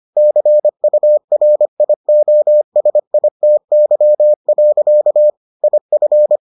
Morse code, heard as a keyed tone, about 25 wpm, 600 Hz. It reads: CURIOSITY. IF